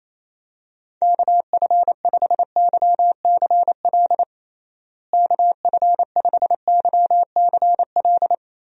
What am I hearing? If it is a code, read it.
Morse code, heard as a keyed tone, about 28 wpm, 700 Hz. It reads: KF5YCL KF5YCL
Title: USA Call Signs - 6 Characters - Repeated 3x - 28wpm